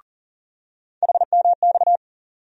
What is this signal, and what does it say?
Morse code, heard as a keyed tone, about 40 wpm, 700 Hz. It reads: HMX
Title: USA Call Sign Postfixes - 3 Letters 40wpm